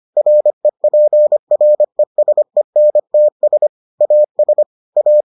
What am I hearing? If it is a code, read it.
Morse code, heard as a keyed tone, about 25 wpm, 600 Hz. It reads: REPRESENTS AS A